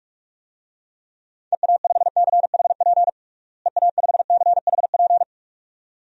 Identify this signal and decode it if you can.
Morse code, heard as a keyed tone, about 46 wpm, 700 Hz. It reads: EA5CHP EA5CHP
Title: DXCC-100 Call Signs - 6 Characters - Repeated 2x - 45wpm